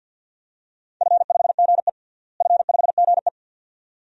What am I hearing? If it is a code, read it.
Morse code, heard as a keyed tone, about 47 wpm, 700 Hz. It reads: F5GE F5GE
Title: DXCC-100 Call Signs - 4 Characters - Repeated 2x - 50wpm